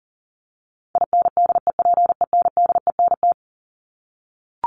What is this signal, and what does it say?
Morse code, heard as a keyed tone, about 40 wpm, 700 Hz. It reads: INDEPENDENT
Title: Top 751-1000 Words 40wpm